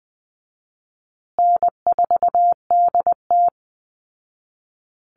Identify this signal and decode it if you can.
Morse code, heard as a keyed tone, about 20 wpm, 700 Hz. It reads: N4DT
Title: Call Signs Repeated 2x - 20wpm